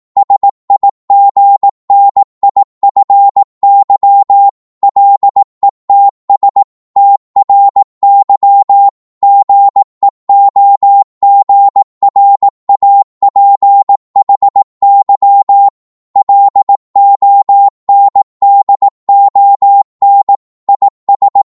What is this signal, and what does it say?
Morse code, heard as a keyed tone, about 18 wpm, 800 Hz. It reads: SIGNIFY LETS TRY GEOGRAPHY LONDON IS